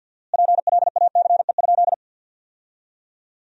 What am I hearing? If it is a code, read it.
Morse code, heard as a keyed tone, about 53 wpm, 700 Hz. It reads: PLACE?
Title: 4x Word-Spacing - Sentences from Top 200 Words - 50wpm